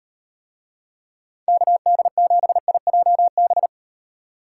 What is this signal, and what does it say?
Morse code, heard as a keyed tone, about 38 wpm, 700 Hz. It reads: KD7IJB